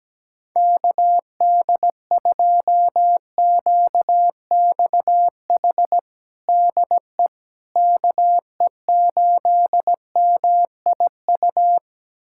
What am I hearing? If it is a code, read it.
Morse code, heard as a keyed tone, about 17 wpm, 700 Hz. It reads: KD2QXH DE KE8MIU
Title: USA Callsign DE Callsign - 6 Characters - 17wpm